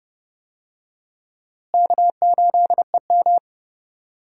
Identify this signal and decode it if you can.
Morse code, heard as a keyed tone, about 30 wpm, 700 Hz. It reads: K8EM
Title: USA Call Signs - 4 Characters - Repeated 3x - 30wpm